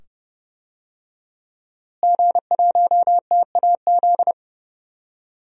Morse code, heard as a keyed tone, about 29 wpm, 700 Hz. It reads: G1TAZ
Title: DXCC-100 Call Signs - 5 Characters - Repeated 2x - 30wpm